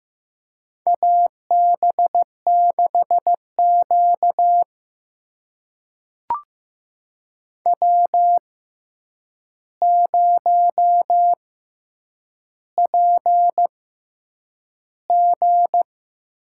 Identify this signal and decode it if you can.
Morse code, heard as a keyed tone, about 15 wpm, spaced as for 8 wpm, 700 Hz. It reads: AB6Q W0PG
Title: USA Call Signs - 4 Characters - 6x Character Spacing - 15wpm